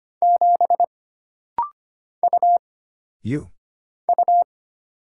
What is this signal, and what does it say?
Morse code, heard as a keyed tone, about 25 wpm, 700 Hz. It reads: 7 U U
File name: Single Character (Letter:Number:Punctuation:Prosign) - Rapid-Fire - 25wpm